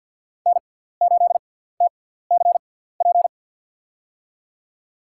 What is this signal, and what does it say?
Morse code, heard as a keyed tone, about 52 wpm, spaced as for 17 wpm, 700 Hz. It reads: N8TCP
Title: USA Call Signs - 5 Characters - 6x Character Spacing - 50wpm